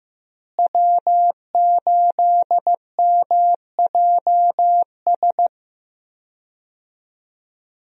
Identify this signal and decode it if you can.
Morse code, heard as a keyed tone, about 15 wpm, 700 Hz. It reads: W8MJS
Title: USA Call Signs - 5 Characters - Repeated 2x - 15wpm